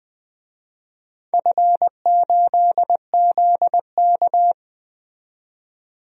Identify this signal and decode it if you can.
Morse code, heard as a keyed tone, about 20 wpm, 700 Hz. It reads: F8ZK
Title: DXCC-100 Call Signs - 4 Characters - 20wpm